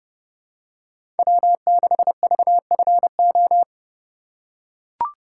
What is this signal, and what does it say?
Morse code, heard as a keyed tone, about 30 wpm, 700 Hz. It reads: W6VFO
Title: USA Call Signs - 5 Characters - 3x Character Spacing - 30wpm